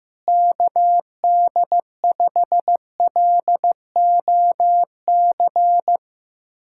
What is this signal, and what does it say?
Morse code, heard as a keyed tone, about 15 wpm, 700 Hz. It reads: KD5LOC
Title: USA Callsign DE Callsign 15wpm